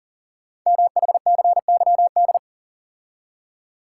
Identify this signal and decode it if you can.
Morse code, heard as a keyed tone, about 42 wpm, 690 Hz. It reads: MHCYD
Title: Code Groups - 5 Letters - 40wpm